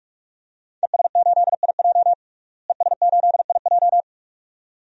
Morse code, heard as a keyed tone, about 46 wpm, 700 Hz. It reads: ES8IJ ES8IJ
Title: DXCC-100 Call Signs - 5 Characters - Repeated 2x - 45wpm